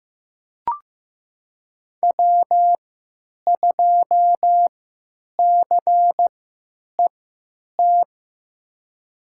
Morse code, heard as a keyed tone, about 15 wpm, spaced as for 9 wpm, 700 Hz. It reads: W2CET
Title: USA Call Signs - 5 Characters - 3x Character Spacing - 15wpm